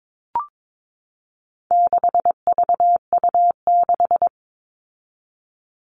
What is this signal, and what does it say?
Morse code, heard as a keyed tone, about 22 wpm, 700 Hz. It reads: 6VU6
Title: All Cut Numbers - 4-Digit 22wpm